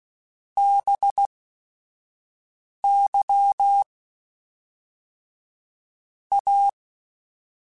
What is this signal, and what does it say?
Morse code, heard as a keyed tone, about 16 wpm, spaced as for 5 wpm, 800 Hz. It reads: BY A
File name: Technical Subjects / Technical Subjects-mp3s / XL @5